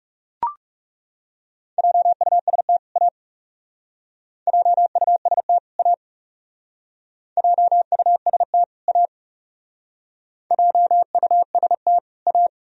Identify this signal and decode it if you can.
Morse code, heard as a keyed tone, about 38 wpm, 700 Hz. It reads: JUST A JUST A JUST A JUST A
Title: Speed-Racing - Sets of 2 Words 30wpm